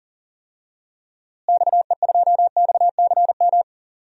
Morse code, heard as a keyed tone, about 40 wpm, 700 Hz. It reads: XE2XCM
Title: DXCC-100 Call Signs - 6 Characters - Repeated 3x - 40wpm